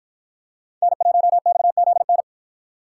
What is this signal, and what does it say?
Morse code, heard as a keyed tone, about 53 wpm, 700 Hz. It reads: N1XZN